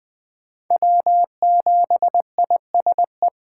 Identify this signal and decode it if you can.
Morse code, heard as a keyed tone, about 24 wpm, 700 Hz. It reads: W7ISE
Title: Speed Racing - USA 1x3 Call Signs 20wpm